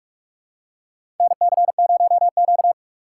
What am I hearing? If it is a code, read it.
Morse code, heard as a keyed tone, about 46 wpm, 700 Hz. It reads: NC0Q
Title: USA Call Signs - 4 Characters - 3x Character Spacing - 45wpm